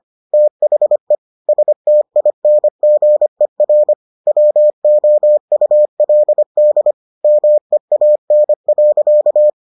Morse code, heard as a keyed tone, about 25 wpm, 600 Hz. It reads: THE STINGER WOULD MEAN.